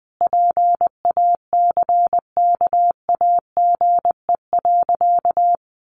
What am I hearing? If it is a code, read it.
Morse code, heard as a keyed tone, about 19 wpm, 700 Hz. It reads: PACKAGE.
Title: Easy Phrases 20wpm